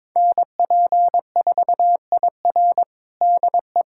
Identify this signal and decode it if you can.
Morse code, heard as a keyed tone, about 22 wpm, 700 Hz. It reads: NP4IR DE
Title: USA Callsign DE Callsign - 5 Characters - 22wpm